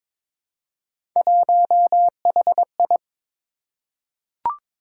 Morse code, heard as a keyed tone, about 23 wpm, 700 Hz. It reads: 1HI
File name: practice / 22wpm / Three Letter-Number Combinations - 22wpm